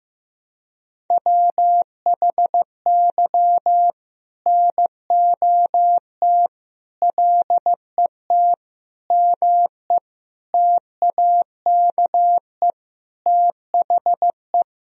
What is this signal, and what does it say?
Morse code, heard as a keyed tone, about 15 wpm, 700 Hz. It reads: WHY NOT LET ME TAKE THE
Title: Sentences from Top 1000 Words - Review Part 2 - 15wpm